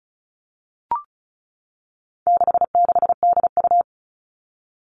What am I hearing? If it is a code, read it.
Morse code, heard as a keyed tone, about 34 wpm, 700 Hz. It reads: 66DU
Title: All Cut Numbers - 4-Digit 35wpm